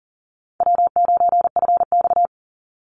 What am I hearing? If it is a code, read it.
Morse code, heard as a keyed tone, about 40 wpm, 700 Hz. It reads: W9FX
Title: Easy Call Signs Repeated 3x - 40wpm